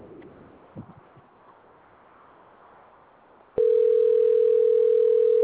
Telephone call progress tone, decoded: ringback tone